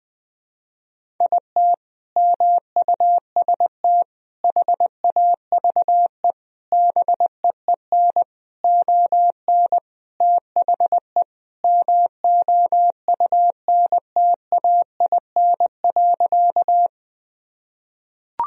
Morse code, heard as a keyed tone, about 21 wpm, 700 Hz. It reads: IT MUST HAVE BEEN ON THE MOUNTAIN.
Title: Sentences from Top 1000 Words - Review Part 1 - 20wpm